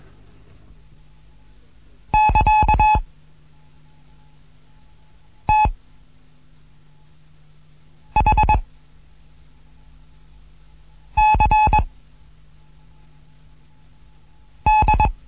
Morse code, heard as a keyed tone, about 21 wpm, 860 Hz. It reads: <KA> T H C D
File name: Morsen / Kurs A 15